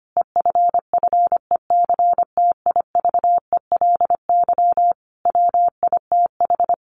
Morse code, heard as a keyed tone, about 25 wpm, 700 Hz. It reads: EFFECTIVELY WITH
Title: Sentences from Top 1400 Words 25wpm